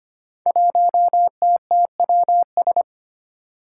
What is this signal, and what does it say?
Morse code, heard as a keyed tone, about 25 wpm, 700 Hz. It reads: 1TTWS
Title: Five Challenging Letter-Number Combinations 25wpm